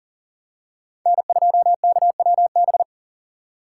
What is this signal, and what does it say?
Morse code, heard as a keyed tone, about 40 wpm, 700 Hz. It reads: N2KWB
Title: USA Call Signs - 5 Characters - 4x Character Spacing - 40wpm